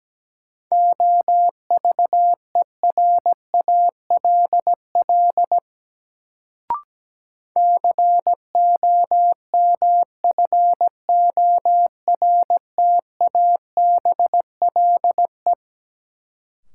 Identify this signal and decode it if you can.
Morse code, heard as a keyed tone, about 17 wpm, 700 Hz. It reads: OVERALL COMFORTABLE
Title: Sentences from Top 1000 Words - Review Part 7 - New Word Drill - 17wpm